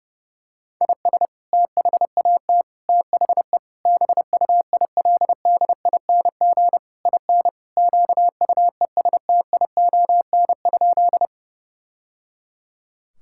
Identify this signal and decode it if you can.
Morse code, heard as a keyed tone, about 30 wpm, 700 Hz. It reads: IS THAT THE BUILDING IN QUESTION?
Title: Sentences from Top 1000 Words - Review Part 5 - 30wpm